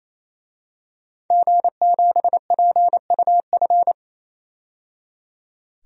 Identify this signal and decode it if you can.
Morse code, heard as a keyed tone, about 28 wpm, 700 Hz. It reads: G7PUF
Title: DXCC-100 Call Signs - 5 Characters - 28wpm